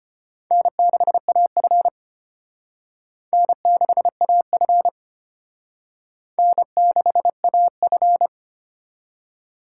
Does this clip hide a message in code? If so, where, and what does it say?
Yes: Morse code from 0.5 to 8.3 s, heard as a keyed tone, about 32 wpm, 700 Hz. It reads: N6AF N6AF N6AF